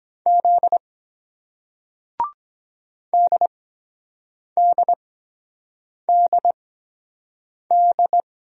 Morse code, heard as a keyed tone, about 23 wpm, spaced as for 7 wpm, 700 Hz. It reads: Z DDDD